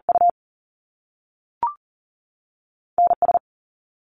Morse code, heard as a keyed tone, about 40 wpm, 700 Hz. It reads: U NS
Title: Canadian Provinces and Territories 40wpm